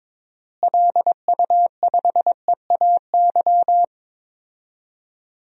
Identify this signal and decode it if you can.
Morse code, heard as a keyed tone, about 22 wpm, 700 Hz. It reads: LU5EAY
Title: DXCC-100 Call Signs - 6 Characters - Repeated 3x - 22wpm